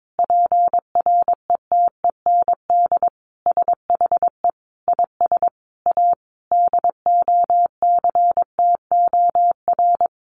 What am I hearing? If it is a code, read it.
Morse code, heard as a keyed tone, about 22 wpm, 700 Hz. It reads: PRETEND SHE IS A DOCTOR
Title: Additional Easy Phrases 22wpm